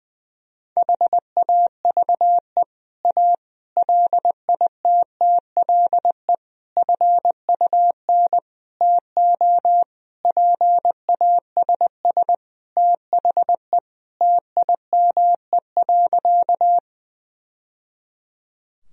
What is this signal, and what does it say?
Morse code, heard as a keyed tone, about 20 wpm, 700 Hz. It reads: HAVE A LITTLE FUN TO PASS THE TIME.
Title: Sentences from Top 1000 Words - Review Part 4 - 20wpm